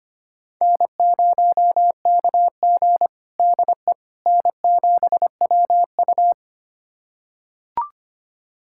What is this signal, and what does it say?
Morse code, heard as a keyed tone, about 25 wpm, 700 Hz. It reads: N0KG DE N7WU
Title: USA Callsign DE Callsign - 4 Characters - 25wpm